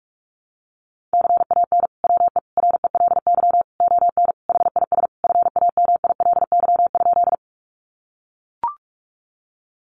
Morse code, heard as a keyed tone, about 44 wpm, 700 Hz. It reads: CAN WE RELY ON HIS FAMILY?
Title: Sentences from Top 1200 Words 45wpm